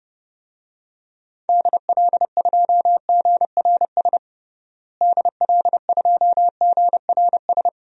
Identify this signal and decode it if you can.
Morse code, heard as a keyed tone, about 30 wpm, 700 Hz. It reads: DL2GRS DL2GRS
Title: DXCC-100 Call Signs - 6 Characters - Repeated 3x - 30wpm